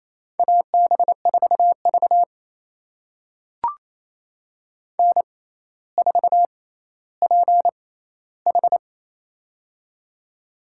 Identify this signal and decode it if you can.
Morse code, heard as a keyed tone, about 28 wpm, spaced as for 14 wpm, 700 Hz. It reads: AB4V N4PH
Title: USA Call Signs - 4 Characters - 6x Character Spacing - 28wpm